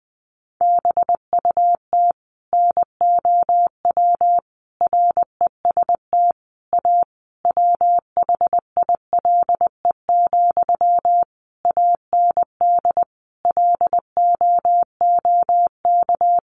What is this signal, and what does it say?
Morse code, heard as a keyed tone, about 19 wpm, 700 Hz. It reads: BUT NOW REST A WHILE, AND LOOK
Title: Sentences from Top 1100 Words 20wpm